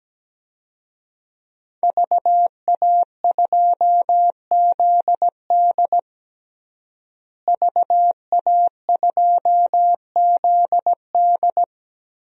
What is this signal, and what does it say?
Morse code, heard as a keyed tone, about 17 wpm, 700 Hz. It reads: VA2ZD VA2ZD
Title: DXCC-100 Call Signs - 5 Characters - Repeated 3x - 17wpm